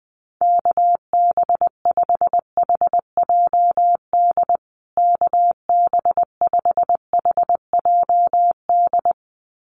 Morse code, heard as a keyed tone, about 20 wpm, 700 Hz. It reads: KB5HJD KB5HJD
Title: USA Call Signs Repeated 2x 20wpm